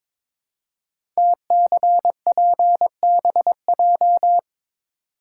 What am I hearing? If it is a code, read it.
Morse code, heard as a keyed tone, about 22 wpm, 700 Hz. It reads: TCPBJ
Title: Code Groups - 5 Letters - 22wpm